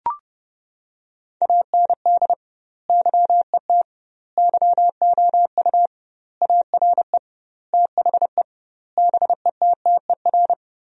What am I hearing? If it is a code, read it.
Morse code, heard as a keyed tone, about 30 wpm, 700 Hz. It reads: AND YET YOU ARE THE BETTER
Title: 2x Word-Spacing - Sentences from Top 500 Words - 30wpm